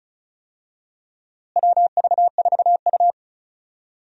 Morse code, heard as a keyed tone, about 36 wpm, 700 Hz. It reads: WV4U
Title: USA Call Signs - 4 Characters - Repeated 3x - 35wpm